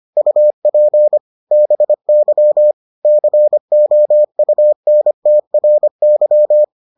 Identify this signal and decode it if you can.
Morse code, heard as a keyed tone, about 25 wpm, 600 Hz. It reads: UP BY COUNTRY